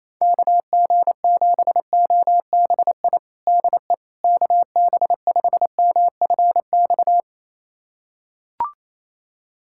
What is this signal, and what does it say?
Morse code, heard as a keyed tone, about 28 wpm, 700 Hz. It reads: KG7OBI DE KB5MFX
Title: USA Callsign DE Callsign - 6 Characters - 28wpm